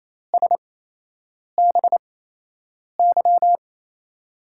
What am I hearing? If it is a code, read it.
Morse code, heard as a keyed tone, about 28 wpm, spaced as for 8 wpm, 700 Hz. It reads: SBY
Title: USA Call Signs - 8x Character Spacing - 28wpm